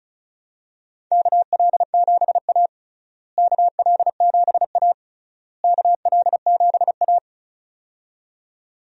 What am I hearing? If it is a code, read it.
Morse code, heard as a keyed tone, about 36 wpm, 700 Hz. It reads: KL7A KL7A KL7A
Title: DXCC-100 Call Signs - 4 Characters - Repeated 3x - 35wpm